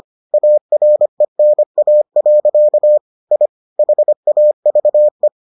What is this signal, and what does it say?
Morse code, heard as a keyed tone, about 25 wpm, 600 Hz. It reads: ARENA. I HAVE